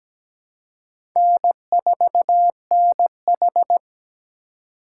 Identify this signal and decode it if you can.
Morse code, heard as a keyed tone, about 17 wpm, 700 Hz. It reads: N4NH
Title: USA Call Signs - 4 Characters - 3x Character Spacing - 17wpm